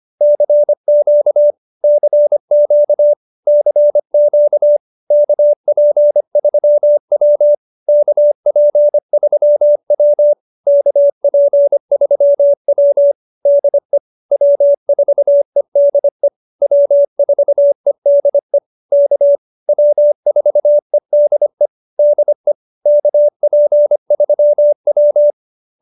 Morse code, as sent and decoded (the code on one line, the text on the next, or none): -.-. --.- / -.-. --.- / -.-. --.- / -.- .--. ...-- .-- / -.- .--. ...-- .-- / -.- .--. ...-- .-- / -.. / .-- ....- . -.. / .-- ....- . -.. / -.- / .-- ....- . -.. / -.. / -.- .--. ...-- .--
CQ CQ CQ KP3W KP3W KP3W DE W4EDE W4EDE K W4EDE DE KP3W